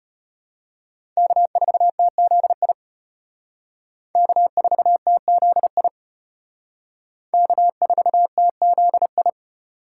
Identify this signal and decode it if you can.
Morse code, heard as a keyed tone, about 32 wpm, 700 Hz. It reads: K4TZI K4TZI K4TZI